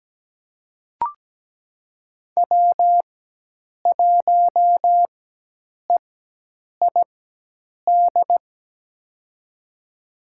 Morse code, heard as a keyed tone, about 17 wpm, spaced as for 8 wpm, 700 Hz. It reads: W1EID
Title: USA Call Signs - 5 Characters - 4x Character Spacing - 17wpm